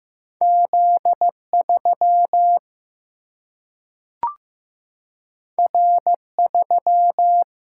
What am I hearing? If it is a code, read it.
Morse code, heard as a keyed tone, about 15 wpm, 700 Hz. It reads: Z 3 R 3
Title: Single-Letter Single-Number 15wpm